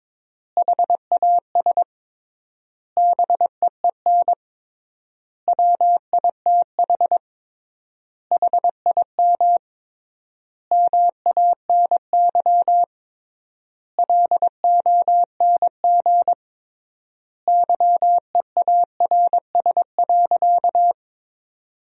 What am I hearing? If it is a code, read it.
Morse code, heard as a keyed tone, about 22 wpm, 700 Hz. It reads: HAS BEEN WITH HIM MANY LONG YEARS.
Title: 3x Word-Spacing - Sentences from Top 200 Words - 22wpm